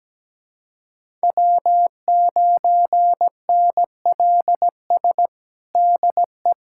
Morse code, heard as a keyed tone, about 17 wpm, 700 Hz. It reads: W9NLS DE
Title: USA Callsign DE Callsign 17wpm